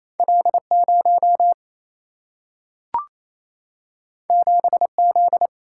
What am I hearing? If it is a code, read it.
Morse code, heard as a keyed tone, about 28 wpm, 700 Hz. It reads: L 0 7 Z
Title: Single-Letter Single-Number 28wpm